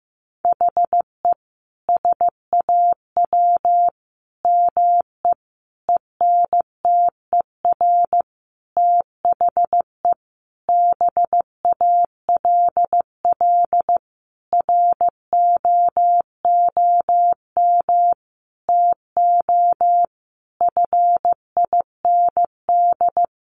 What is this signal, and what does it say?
Morse code, heard as a keyed tone, about 15 wpm, 700 Hz. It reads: HE SAW ME ENTER THE BALL ROOM TO FIND
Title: Sentences from Top 1300 Words 15wpm